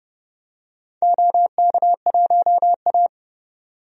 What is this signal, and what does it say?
Morse code, heard as a keyed tone, about 30 wpm, 700 Hz. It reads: OK1A
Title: DXCC-100 Call Signs - 4 Characters - 30wpm